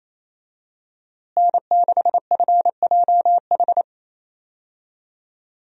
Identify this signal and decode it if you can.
Morse code, heard as a keyed tone, about 28 wpm, 700 Hz. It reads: N6FJH
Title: USA Call Signs - 5 Characters - Repeated 3x - 28wpm